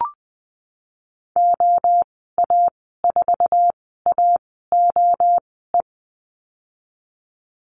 Morse code, heard as a keyed tone, about 20 wpm, spaced as for 14 wpm, 700 Hz. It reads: OA4AOE
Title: 055 Lesson 10g - Letter r - Call Signs 20wpm